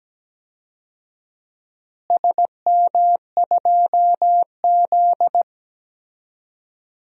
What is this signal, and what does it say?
Morse code, heard as a keyed tone, about 17 wpm, 700 Hz. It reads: SM2Z
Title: DXCC-100 Call Signs - 4 Characters - Repeated 3x - 17wpm